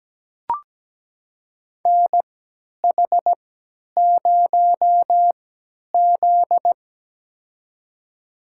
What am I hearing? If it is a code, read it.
Morse code, heard as a keyed tone, about 17 wpm, spaced as for 10 wpm, 700 Hz. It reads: NH0Z
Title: USA Call Signs - 4 Characters - 3x Character Spacing - 17wpm